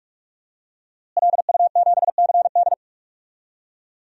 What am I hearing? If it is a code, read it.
Morse code, heard as a keyed tone, about 46 wpm, 700 Hz. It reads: LU7CD
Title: DXCC-100 Call Signs - 5 Characters - Repeated 3x - 45wpm